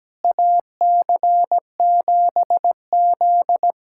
Morse code, heard as a keyed tone, about 17 wpm, 700 Hz. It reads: AC7Z
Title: USA Callsign DE Callsign - 4 Characters - 17wpm